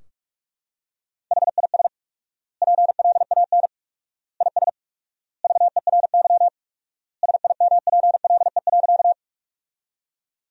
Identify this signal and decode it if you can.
Morse code, heard as a keyed tone, about 46 wpm, 700 Hz. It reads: HIS PLAN IS VERY SIMPLE.
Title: 4x Word-Spacing - Sentences from Top 500 Words - 45wpm